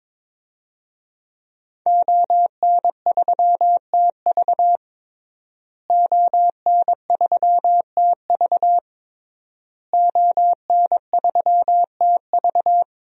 Morse code, heard as a keyed tone, about 22 wpm, 700 Hz. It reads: ON3TV ON3TV ON3TV
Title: DXCC-100 Call Signs - 5 Characters - Repeated 3x - 22wpm